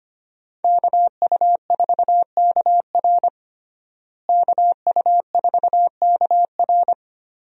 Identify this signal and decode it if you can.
Morse code, heard as a keyed tone, about 25 wpm, 700 Hz. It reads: KU4KR KU4KR
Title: USA Call Signs - 5 Characters - Repeated 3x - 25wpm